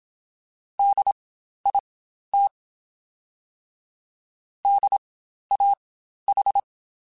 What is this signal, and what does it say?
Morse code, heard as a keyed tone, about 27 wpm, spaced as for 12 wpm, 800 Hz. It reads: DIT DAH
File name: Spaced Alphas / Spaced Alphas mp3s / Ds sp @25